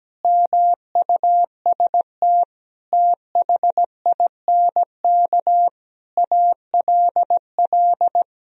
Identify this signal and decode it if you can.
Morse code, heard as a keyed tone, about 17 wpm, 700 Hz. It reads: MUST THINK ALL
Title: Sentences from Top 1000 Words - Review Part 3 - 17wpm